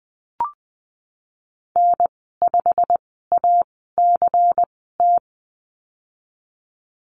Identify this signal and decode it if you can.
Morse code, heard as a keyed tone, about 20 wpm, spaced as for 15 wpm, 700 Hz. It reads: N5ACT
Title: 186 Lesson 30g - Letter b - Call Signs 20wpm